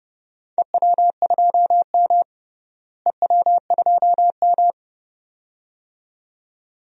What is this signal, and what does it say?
Morse code, heard as a keyed tone, about 30 wpm, 700 Hz. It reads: EW2M EW2M
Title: DXCC-100 Call Signs - 4 Characters - Repeated 2x - 30wpm